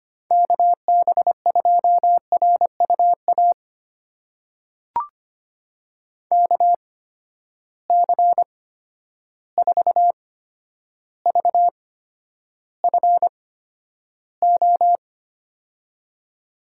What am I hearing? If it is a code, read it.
Morse code, heard as a keyed tone, about 25 wpm, spaced as for 11 wpm, 700 Hz. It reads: KB2RUA KC4VFO
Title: USA Call Signs - 8x Character Spacing - 25wpm